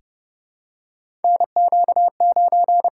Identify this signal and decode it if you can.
Morse code, heard as a keyed tone, about 30 wpm, 700 Hz. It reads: NQ9
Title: USA Call Signs - 4 Characters - 8x Character Spacing - 30wpm